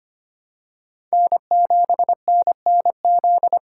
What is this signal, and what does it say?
Morse code, heard as a keyed tone, about 25 wpm, 700 Hz. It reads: N7NNZ
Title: USA Call Signs - 4x Character Spacing - 25wpm